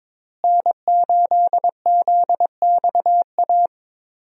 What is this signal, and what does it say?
Morse code, heard as a keyed tone, about 22 wpm, 700 Hz. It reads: N8ZXA
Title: USA Call Signs - 5 Characters - Repeated 3x - 22wpm